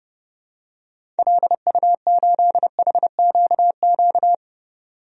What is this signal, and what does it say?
Morse code, heard as a keyed tone, about 32 wpm, 700 Hz. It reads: LU8HQQ
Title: DXCC-100 Call Signs - 6 Characters - 30wpm